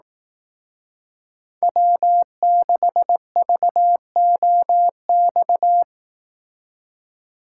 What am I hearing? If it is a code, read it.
Morse code, heard as a keyed tone, about 18 wpm, 700 Hz. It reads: W6VOX
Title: Speed Racing - USA 1x3 Call Signs 15wpm